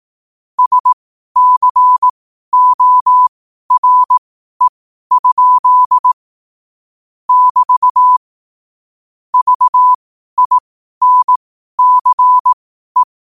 Morse code, heard as a keyed tone, about 18 wpm, spaced as for 12 wpm, 1000 Hz. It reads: SCORE? <BT> VINCE
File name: qotdfiles / 12wpm